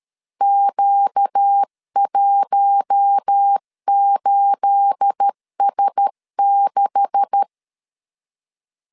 Morse code, heard as a keyed tone, about 13 wpm, 800 Hz. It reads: Q18S6
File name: LEZIONI / 9 286 12wpm 2305